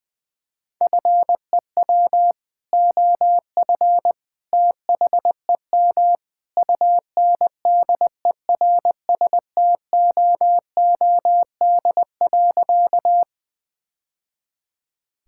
Morse code, heard as a keyed tone, about 21 wpm, 700 Hz. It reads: FEW OF THEM UNDERSTOOD.
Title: Sentences with 2-Word Phrases - Spoken Beginning - 20wpm